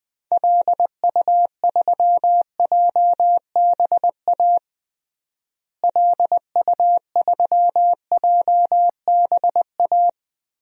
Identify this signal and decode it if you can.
Morse code, heard as a keyed tone, about 20 wpm, 700 Hz. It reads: LU3JBA LU3JBA
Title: DXCC-100 Call Signs - 6 Characters - Repeated 2x - 20wpm